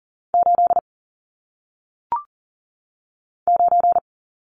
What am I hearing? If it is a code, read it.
Morse code, heard as a keyed tone, about 38 wpm, 700 Hz. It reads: 8 9
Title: Numbers - 1-Digit 40wpm